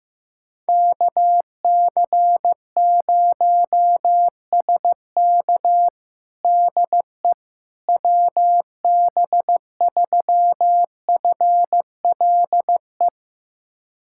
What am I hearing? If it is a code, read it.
Morse code, heard as a keyed tone, about 15 wpm, 700 Hz. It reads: KC0SK DE WB3FLE
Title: USA Callsign DE Callsign 15wpm